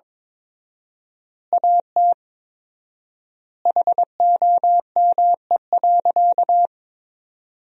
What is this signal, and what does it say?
Morse code, heard as a keyed tone, about 22 wpm, 700 Hz. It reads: AT HOME.
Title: 4x Word-Spacing - Sentences from Top 400 Words - 22wpm